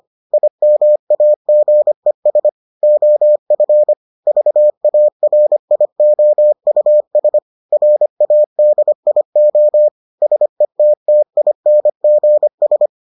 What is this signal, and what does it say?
Morse code, heard as a keyed tone, about 25 wpm, 600 Hz. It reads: IMAGES OF VARIOUS RADIO SETTINGS